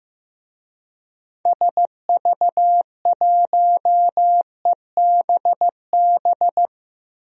Morse code, heard as a keyed tone, about 15 wpm, 700 Hz. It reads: SV1EBB
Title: DXCC-100 Call Signs - 6 Characters - 15wpm